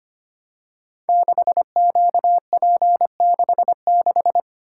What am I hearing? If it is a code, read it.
Morse code, heard as a keyed tone, about 25 wpm, 700 Hz. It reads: 6QP66
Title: Code Groups - 5 Letter:Numbers - 25wpm